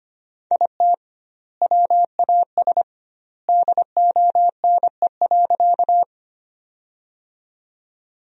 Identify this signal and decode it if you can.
Morse code, heard as a keyed tone, about 25 wpm, 700 Hz. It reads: IT WAS DONE.
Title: 2x Word-Spacing - Sentences from Top 500 Words - 25wpm